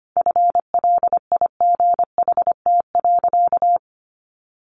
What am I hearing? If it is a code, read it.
Morse code, heard as a keyed tone, about 25 wpm, 700 Hz. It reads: FLIGHT.
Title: Sentences from Top 1400 Words 25wpm